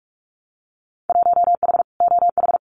Morse code, heard as a keyed tone, about 44 wpm, 700 Hz. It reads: 1H OH
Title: Winter Field Day Exchanges 45wpm